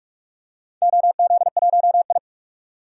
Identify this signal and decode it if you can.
Morse code, heard as a keyed tone, about 47 wpm, 700 Hz. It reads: OZ1I
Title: DXCC-100 Call Signs - 4 Characters - 45wpm